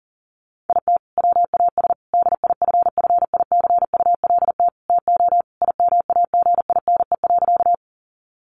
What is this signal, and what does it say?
Morse code, heard as a keyed tone, about 40 wpm, 700 Hz. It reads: IT WAS DIFFICULT TO IMAGINE.
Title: Sentences from Top 1400 Words 40wpm